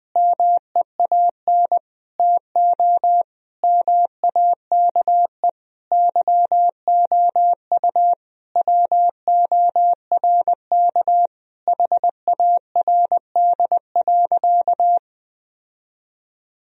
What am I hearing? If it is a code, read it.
Morse code, heard as a keyed tone, about 21 wpm, 700 Hz. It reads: MEAN TO MAKE YOU WORK HARD.
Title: Sentences from Top 1000 Words - Review Part 1 - 20wpm